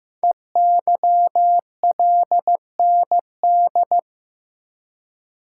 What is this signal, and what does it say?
Morse code, heard as a keyed tone, about 15 wpm, 700 Hz. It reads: EYLND
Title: Code Groups - 5 Letters - 15wpm